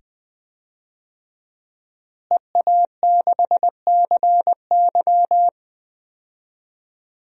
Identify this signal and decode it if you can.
Morse code, heard as a keyed tone, about 20 wpm, 700 Hz. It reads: EA6CY
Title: DXCC-100 Call Signs - 5 Characters - Repeated 2x - 20wpm